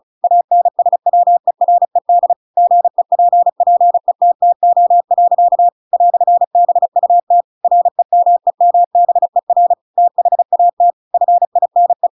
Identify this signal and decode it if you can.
Morse code, heard as a keyed tone, about 34 wpm, 700 Hz. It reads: ANSWERED GEPPETTO. "BUT REMEMBER THAT FINE
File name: Pinocchio / Pinocchio ch8 35 35 ik1hge